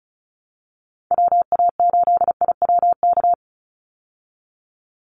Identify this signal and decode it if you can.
Morse code, heard as a keyed tone, about 34 wpm, 700 Hz. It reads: WA8IWK
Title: Call Signs Repeated 2x - 35wpm